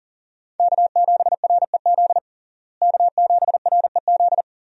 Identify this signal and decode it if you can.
Morse code, heard as a keyed tone, about 40 wpm, 700 Hz. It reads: K7REZ K7REZ
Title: USA Call Signs - 5 Characters - Repeated 2x - 40wpm